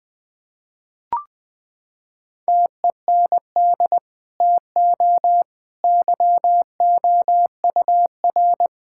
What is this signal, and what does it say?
Morse code, heard as a keyed tone, about 20 wpm, 700 Hz. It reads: TEND TO YOUR
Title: Sentences from Top 1000 Words - Review Part 5 - 20wpm